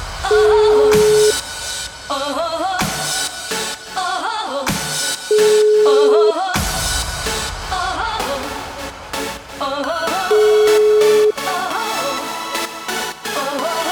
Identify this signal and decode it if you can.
Ringback tone (Europe, 425 Hz), with noise.